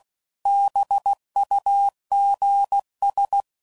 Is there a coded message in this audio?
Yes: Morse code from 0.5 to 3.4 s, heard as a keyed tone, about 16 wpm, 800 Hz. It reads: BUGS